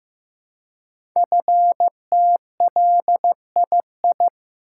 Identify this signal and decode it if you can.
Morse code, heard as a keyed tone, about 15 wpm, 700 Hz. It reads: FTLII
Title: Code Groups - 5 Letters - 15wpm